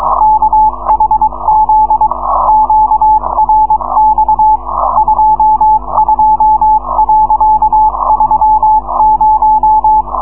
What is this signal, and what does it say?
Morse code, heard as a keyed tone, about 22 wpm, 860 Hz. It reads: KHZ QRX 22.30